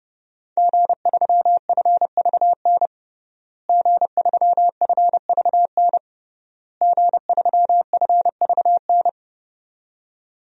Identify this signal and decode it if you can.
Morse code, heard as a keyed tone, about 30 wpm, 700 Hz. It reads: G3FVN G3FVN G3FVN
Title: DXCC-100 Call Signs - 5 Characters - Repeated 3x - 30wpm